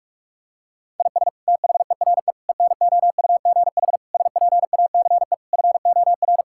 Morse code, heard as a keyed tone, about 46 wpm, 700 Hz. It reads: IS THERE ENOUGH SPACE FOR
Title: Sentences from Top 1000 Words - Review Part 5 - 45wpm